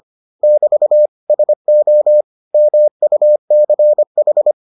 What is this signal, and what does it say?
Morse code, heard as a keyed tone, about 25 wpm, 600 Hz. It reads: <BT> SO MUCH